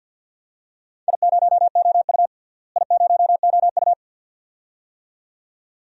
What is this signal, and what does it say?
Morse code, heard as a keyed tone, about 47 wpm, 700 Hz. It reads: I0OU I0OU
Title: DXCC-100 Call Signs - 4 Characters - Repeated 2x - 50wpm